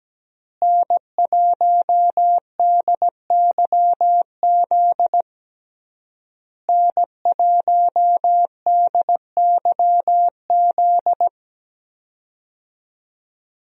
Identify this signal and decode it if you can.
Morse code, heard as a keyed tone, about 17 wpm, 700 Hz. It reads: N1DYZ N1DYZ
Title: USA Call Signs - 5 Characters - Repeated 3x - 17wpm